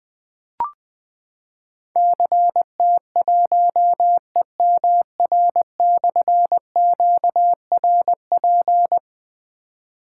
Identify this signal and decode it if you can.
Morse code, heard as a keyed tone, about 21 wpm, 700 Hz. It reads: CT1EMR/QRP
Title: Difficult Call Signs 20wpm